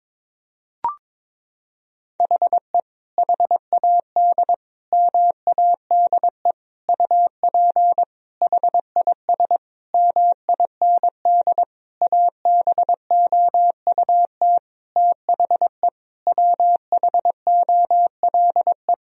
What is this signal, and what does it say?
Morse code, heard as a keyed tone, about 22 wpm, 700 Hz. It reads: HE HAD MADE UP HIS MIND ABOUT THE WHOLE
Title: Sentences from Top 1000 Words - Review Part 3 - 22wpm